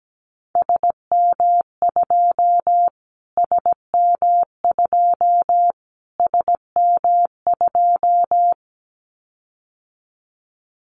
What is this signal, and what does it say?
Morse code, heard as a keyed tone, about 17 wpm, 700 Hz. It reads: SM2 SM2 SM2
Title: Callsign Prefixes for Top 25 DXCC - Repeated 3x - 17wpm